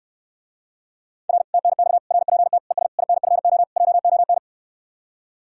Morse code, heard as a keyed tone, about 69 wpm, 700 Hz. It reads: R TNX RPT ES INFO JON